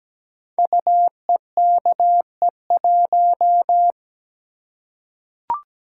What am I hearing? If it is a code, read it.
Morse code, heard as a keyed tone, about 17 wpm, 700 Hz. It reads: UEKE1
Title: Code Groups - 5 Letter:Numbers - 17wpm